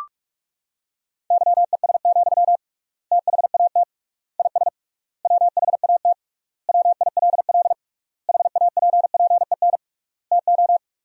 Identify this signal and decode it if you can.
Morse code, heard as a keyed tone, about 46 wpm, 700 Hz. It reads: YES, THAT IS WHAT WILL HAPPEN TO
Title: 3x Word-Spacing - Sentences from Top 500 Words - 45wpm